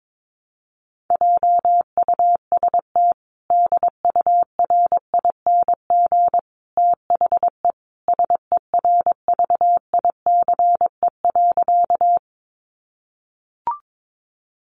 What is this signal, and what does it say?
Morse code, heard as a keyed tone, about 22 wpm, 700 Hz. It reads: JUST DURING THE SERVICE.
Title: Sentences from Top 700 Words 22wpm